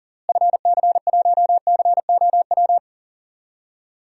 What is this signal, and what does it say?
Morse code, heard as a keyed tone, about 40 wpm, 700 Hz. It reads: FC1COW
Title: DXCC-100 Call Signs - 6 Characters - Repeated 3x - 40wpm